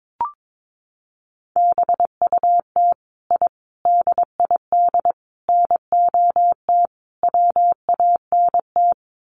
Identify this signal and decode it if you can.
Morse code, heard as a keyed tone, about 22 wpm, 700 Hz. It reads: BUT I DID NOT WANT
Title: Sets of 5 Words Encore 22wpm